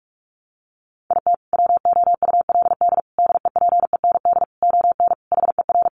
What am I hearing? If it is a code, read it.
Morse code, heard as a keyed tone, about 44 wpm, 700 Hz. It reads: IT WOULD DEPEND ON HER
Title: Sentences from Top 1300 Words 45wpm